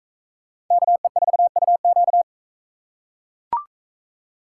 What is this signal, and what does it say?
Morse code, heard as a keyed tone, about 42 wpm, 700 Hz. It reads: KE4UQ